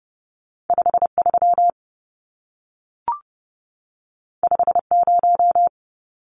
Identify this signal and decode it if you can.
Morse code, heard as a keyed tone, about 30 wpm, 700 Hz. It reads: 53 50
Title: Numbers - 2-Digit 30wpm